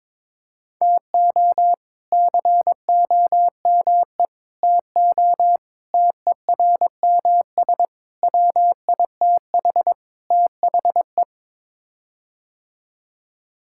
Morse code, heard as a keyed tone, about 22 wpm, 700 Hz. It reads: TO COME TO TERMS WITH THE
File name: practice / 22wpm / Sets of 6 Words - 22wpm